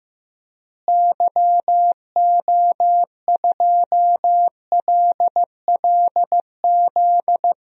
Morse code, heard as a keyed tone, about 15 wpm, 700 Hz. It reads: YO2LLZ
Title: DXCC-100 Call Signs - 6 Characters - Repeated 2x - 15wpm